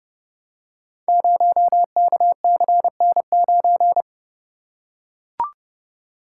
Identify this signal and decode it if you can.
Morse code, heard as a keyed tone, about 29 wpm, 700 Hz. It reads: 0KCN9
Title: Code Groups - 5 Letter:Numbers - 30wpm